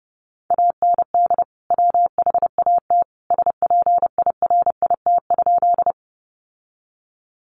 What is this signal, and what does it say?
Morse code, heard as a keyed tone, about 30 wpm, 700 Hz. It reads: AND WHAT SPIRIT?
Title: Sentences from Top 1400 Words 30wpm